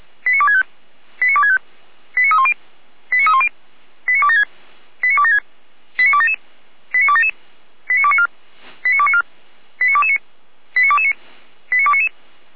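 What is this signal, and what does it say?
Five-tone selective call (ZVEI), heard as quick runs of five tones, tones of about 70 ms. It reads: 89265 89265 89310 89310 89276 89276 89277 89277 89294 89294 89229 89229 89299
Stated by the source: ZVEI five-tone call received by SDR, archived by Feuerwehr Orlamünde.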